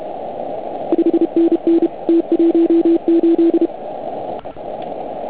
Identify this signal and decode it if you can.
Morse code, heard as a keyed tone, about 32 wpm, 330 Hz. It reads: 5NN T18
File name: Archiv / CUC149 / OK1IF CUC149